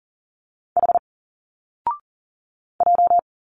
Morse code, heard as a keyed tone, about 40 wpm, 700 Hz. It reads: H J
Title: Single Letters 40wpm